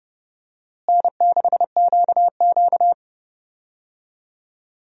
Morse code, heard as a keyed tone, about 30 wpm, 700 Hz. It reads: N6QQ
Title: USA Call Signs - 4 Characters - Repeated 2x - 30wpm